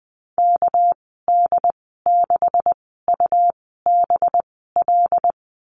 Morse code, heard as a keyed tone, about 20 wpm, spaced as for 14 wpm, 700 Hz. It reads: KD6UBL
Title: 245 Lesson 39g - Letter z - Call Signs 20wpm